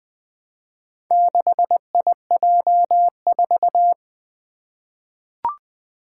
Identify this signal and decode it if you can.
Morse code, heard as a keyed tone, about 20 wpm, 700 Hz. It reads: 6IJ4
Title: Four Challenging Letter-Number Combinations 20wpm